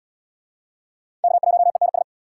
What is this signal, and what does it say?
Morse code, heard as a keyed tone, about 72 wpm, 700 Hz. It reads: K1ENS